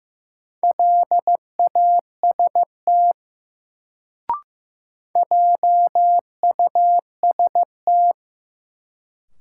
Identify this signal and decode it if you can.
Morse code, heard as a keyed tone, about 15 wpm, 700 Hz. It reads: LAST JUST
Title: Sets of 2 Words - New Word Drill - 15wpm